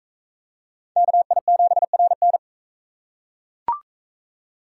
Morse code, heard as a keyed tone, about 41 wpm, 700 Hz. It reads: KI7RN